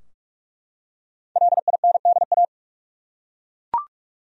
Morse code, heard as a keyed tone, about 48 wpm, 700 Hz. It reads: LINDA
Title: USA Common Names - 45wpm